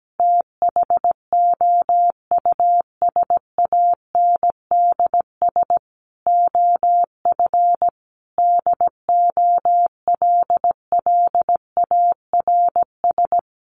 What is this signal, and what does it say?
Morse code, heard as a keyed tone, about 17 wpm, 700 Hz. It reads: THOUSANDS OF DOLLARS
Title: Sets of 5 Words Encore 17wpm